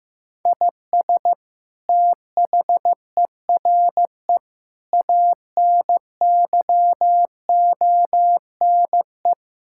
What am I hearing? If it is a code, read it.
Morse code, heard as a keyed tone, about 15 wpm, 700 Hz. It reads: IS THERE ANYONE
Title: Sentences from Top 1000 Words - Review Part 6 - 15wpm